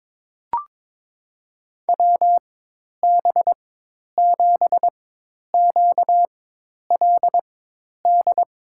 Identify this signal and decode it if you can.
Morse code, heard as a keyed tone, about 22 wpm, spaced as for 10 wpm, 700 Hz. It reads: WB7QLD
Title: USA Call Signs - 6 Characters - 4x Character Spacing - 22wpm